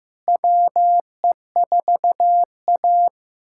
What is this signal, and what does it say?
Morse code, heard as a keyed tone, about 16 wpm, 700 Hz. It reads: WE4A
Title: USA Callsign DE Callsign - 4 Characters - 15wpm